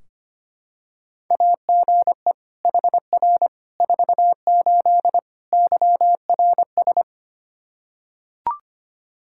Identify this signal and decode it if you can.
Morse code, heard as a keyed tone, about 25 wpm, 700 Hz. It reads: AGE HR 48 YRS
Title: Practice QSO - Exchange 3 - More Info - 25wpm